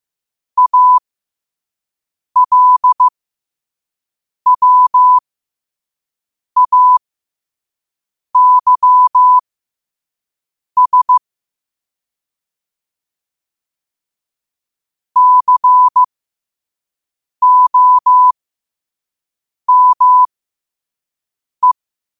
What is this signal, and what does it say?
Morse code, heard as a keyed tone, about 15 wpm, spaced as for 5 wpm, 1000 Hz. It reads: ALWAYS COME
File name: qotdfiles / 5wpm